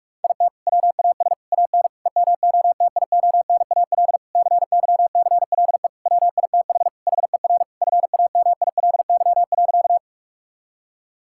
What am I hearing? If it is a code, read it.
Morse code, heard as a keyed tone, about 46 wpm, 700 Hz. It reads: IT WAS AN EMOTIONAL CYCLE WITH HER FAMILY.
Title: Sentences from Top 1000 Words - Review Part 6 - 45wpm